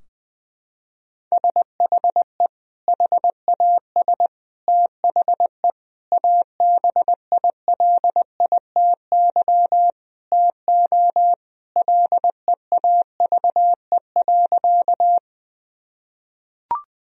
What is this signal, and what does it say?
Morse code, heard as a keyed tone, about 20 wpm, 700 Hz. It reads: SHE HAS THE ABILITY TO LEAVE.
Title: Sentences from Top 1000 Words - Review Part 3 - 20wpm